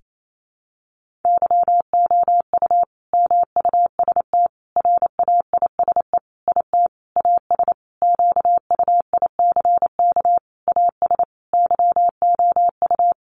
Morse code, heard as a keyed tone, about 28 wpm, 700 Hz. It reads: YOU MUST RAISE IT AS QUICK AS YOU
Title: Sentences from Top 1400 Words 28wpm